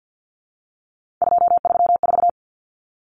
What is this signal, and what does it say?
Morse code, heard as a keyed tone, about 50 wpm, 700 Hz. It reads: 234
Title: Numbers - 3-Digit 50wpm